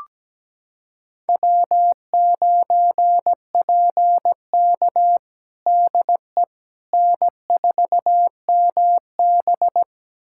Morse code, heard as a keyed tone, about 17 wpm, 700 Hz. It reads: W9PK DE N4MB
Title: USA Callsign DE Callsign - 4 Characters - 17wpm